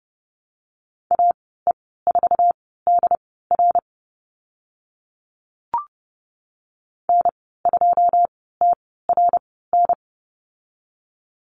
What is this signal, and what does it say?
Morse code, heard as a keyed tone, about 30 wpm, spaced as for 17 wpm, 700 Hz. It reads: AE4DR N2TRN
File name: Lesson 15g - Number 5 - Call Signs 30wpm